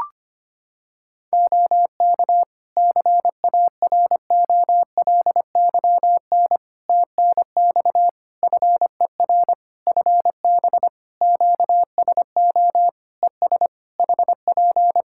Morse code, heard as a keyed tone, about 25 wpm, 700 Hz. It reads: OK CAROLYN TNX FER FB QSO ES HP
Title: Practice QSO - Exchange 4 - Ending - 25wpm